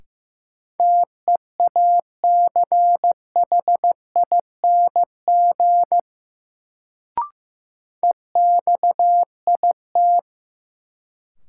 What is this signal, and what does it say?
Morse code, heard as a keyed tone, about 15 wpm, 700 Hz. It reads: TEACHING EXIT
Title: Sentences from Top 1100 Words - New Word Drill Alt - 15wpm